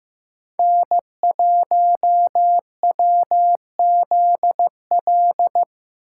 Morse code, heard as a keyed tone, about 18 wpm, 700 Hz. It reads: N1WZL
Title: Speed Racing - USA 1x3 Call Signs 15wpm